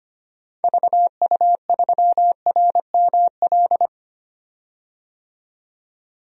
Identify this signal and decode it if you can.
Morse code, heard as a keyed tone, about 25 wpm, 700 Hz. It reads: VU3RML
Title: DXCC-100 Call Signs - 6 Characters - 25wpm